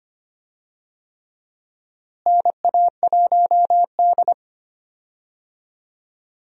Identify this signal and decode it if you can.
Morse code, heard as a keyed tone, about 25 wpm, 700 Hz. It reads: NA1D
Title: USA Call Signs - 4 Characters - Repeated 3x - 25wpm